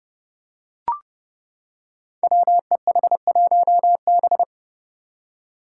Morse code, heard as a keyed tone, about 30 wpm, 700 Hz. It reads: WEH1B
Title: Five Challenging Letter-Number Combinations 30wpm